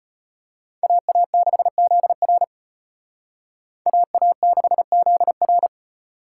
Morse code, heard as a keyed tone, about 30 wpm, 700 Hz. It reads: AA6ZR AA6ZR
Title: Speed Racing - USA Call Signs 25wpm